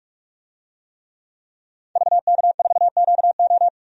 Morse code, heard as a keyed tone, about 45 wpm, 700 Hz. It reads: VK4QO
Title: DXCC-100 Call Signs - 5 Characters - 45wpm